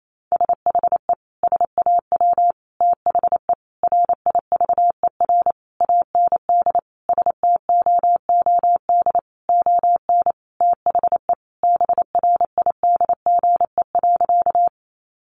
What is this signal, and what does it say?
Morse code, heard as a keyed tone, about 28 wpm, 700 Hz. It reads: SHE SAW THE RIVER AND STOOD ON THE BRIDGE.
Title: Sentences from Top 1200 Words 28wpm